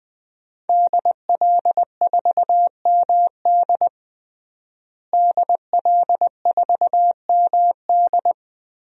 Morse code, heard as a keyed tone, about 20 wpm, 700 Hz. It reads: DL4MD DL4MD
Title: DXCC-100 Call Signs - 5 Characters - Repeated 3x - 20wpm